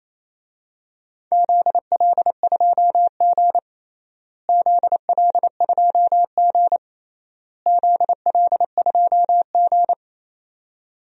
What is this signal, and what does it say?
Morse code, heard as a keyed tone, about 28 wpm, 700 Hz. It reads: ZL2G ZL2G ZL2G
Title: DXCC-100 Call Signs - 4 Characters - Repeated 3x - 28wpm